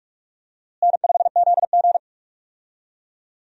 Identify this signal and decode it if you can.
Morse code, heard as a keyed tone, about 41 wpm, 700 Hz. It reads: N5ZG
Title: USA Call Signs - 4 Characters - Repeated 2x - 45wpm